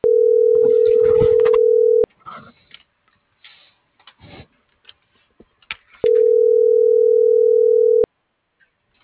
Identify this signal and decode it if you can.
Ringback tone (North America), 440 + 480 Hz.